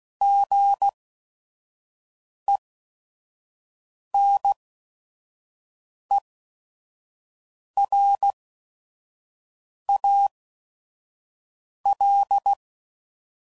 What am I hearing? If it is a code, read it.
Morse code, heard as a keyed tone, about 16 wpm, spaced as for 5 wpm, 800 Hz. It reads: GENERAL
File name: CW Code Groups @5WPM / fromSM's / Classes